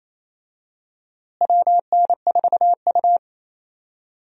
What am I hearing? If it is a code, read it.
Morse code, heard as a keyed tone, about 28 wpm, 700 Hz. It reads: WN4U
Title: USA Call Signs - 4 Characters - Repeated 2x - 28wpm